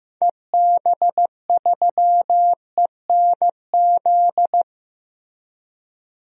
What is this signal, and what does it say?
Morse code, heard as a keyed tone, about 16 wpm, 700 Hz. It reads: EB3ENZ
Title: DXCC-100 Call Signs - 6 Characters - Repeated 2x - 15wpm